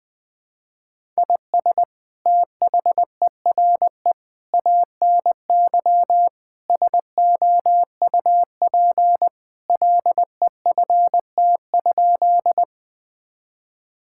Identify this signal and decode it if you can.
Morse code, heard as a keyed tone, about 20 wpm, 700 Hz. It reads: IS THERE ANY SOUP LEFT?
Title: Sentences from Top 1000 Words - Review Part 6 - 20wpm